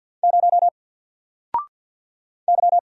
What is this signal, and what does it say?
Morse code, heard as a keyed tone, about 50 wpm, 700 Hz. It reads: 0 Y
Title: Single Character (Letter:Number:Punctuation:Prosign) 50wpm